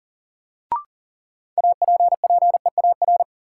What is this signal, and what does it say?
Morse code, heard as a keyed tone, about 40 wpm, 700 Hz. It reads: APPEAR
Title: Sentences from Top 1200 Words - New Word Drill Alt - 40wpm